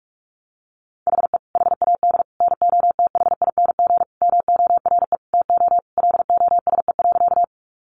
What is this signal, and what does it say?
Morse code, heard as a keyed tone, about 44 wpm, 700 Hz. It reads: HE HAD NOTHING MORE TO LOSE.
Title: Sentences from Top 1100 Words 45wpm